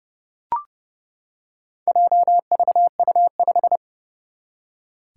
Morse code, heard as a keyed tone, about 30 wpm, 700 Hz. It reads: JVU5
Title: Four Challenging Letter-Number Combinations 30wpm